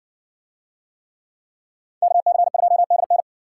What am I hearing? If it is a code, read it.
Morse code, heard as a keyed tone, about 60 wpm, 700 Hz. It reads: KC2DN